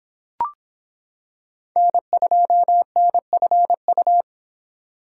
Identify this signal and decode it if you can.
Morse code, heard as a keyed tone, about 26 wpm, 700 Hz. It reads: N2NFU